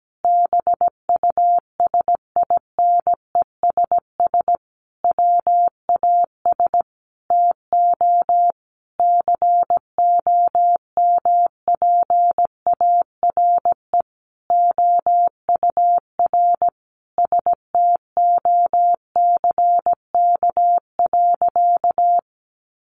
Morse code, heard as a keyed tone, about 17 wpm, 700 Hz. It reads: BUSINESS WAS TO COMPARE OUR STOCK.
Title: Sentences from Top 1300 Words 17wpm